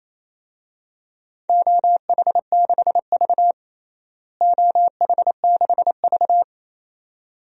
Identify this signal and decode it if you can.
Morse code, heard as a keyed tone, about 28 wpm, 700 Hz. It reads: OH6V OH6V
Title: DXCC-100 Call Signs - 4 Characters - Repeated 2x - 28wpm